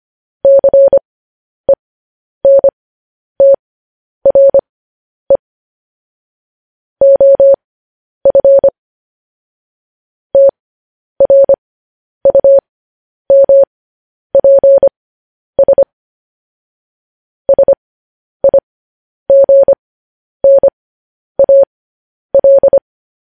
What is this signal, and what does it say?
Morse code, heard as a keyed tone, about 25 wpm, spaced as for 10 wpm, 550 Hz. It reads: CENTRE OF TRUMPS SIGNAL
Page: news-headlines-in-morse-code-at-10-wpm